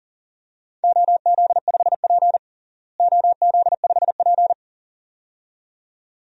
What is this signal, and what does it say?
Morse code, heard as a keyed tone, about 40 wpm, 700 Hz. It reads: OZ5P OZ5P
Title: DXCC-100 Call Signs - 4 Characters - Repeated 2x - 40wpm